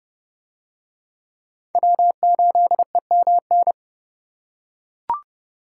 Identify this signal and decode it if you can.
Morse code, heard as a keyed tone, about 27 wpm, 700 Hz. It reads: W8EMN